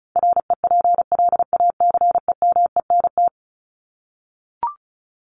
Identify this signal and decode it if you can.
Morse code, heard as a keyed tone, about 34 wpm, 700 Hz. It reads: REPLACEMENT
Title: Top 1751-2000 Words 35wpm